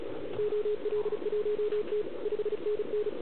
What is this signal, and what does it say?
Morse code, heard as a keyed tone, about 36 wpm, 410 Hz, with radio noise. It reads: OL1T 5NN